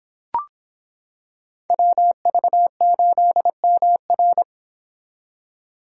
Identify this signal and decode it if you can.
Morse code, heard as a keyed tone, about 26 wpm, 700 Hz. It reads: WV8MR